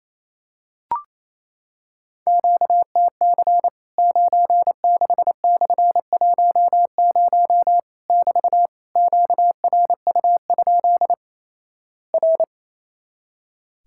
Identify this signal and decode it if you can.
Morse code, heard as a keyed tone, about 28 wpm, 700 Hz. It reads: QTC 96/10 <BT> QRU? R
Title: Worked All Europe DX Contest - QTC Exchange - 28wpm